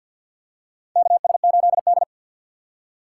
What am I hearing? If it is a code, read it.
Morse code, heard as a keyed tone, about 48 wpm, 700 Hz. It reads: KS8D
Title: USA Call Signs - 4 Characters - 3x Character Spacing - 50wpm